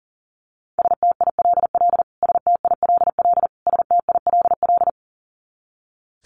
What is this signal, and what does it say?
Morse code, heard as a keyed tone, about 37 wpm, 700 Hz. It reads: STILL STILL STILL
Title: Top 500 Words Repeated 40wpm